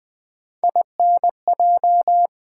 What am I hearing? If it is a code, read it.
Morse code, heard as a keyed tone, about 20 wpm, 700 Hz. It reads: INJ
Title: Build A Word - 3 Letters At A Time - 6-Letter Words - 20wpm